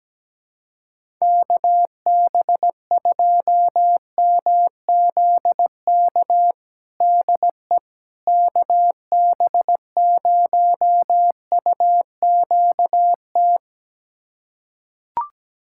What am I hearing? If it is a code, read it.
Morse code, heard as a keyed tone, about 17 wpm, 700 Hz. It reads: KB2MZK DE KB0UQT
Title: USA Callsign DE Callsign 17wpm